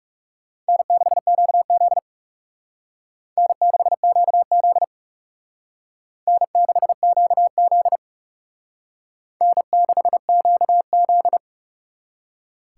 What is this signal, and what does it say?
Morse code, heard as a keyed tone, about 34 wpm, 700 Hz. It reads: N6QZ N6QZ N6QZ N6QZ
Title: Speed Racing - USA 1x2 Call Signs 30wpm